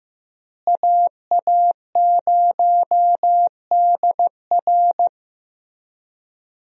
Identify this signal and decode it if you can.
Morse code, heard as a keyed tone, about 15 wpm, 700 Hz. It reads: AA0DR
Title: USA Call Signs - 5 Characters - Repeated 2x - 15wpm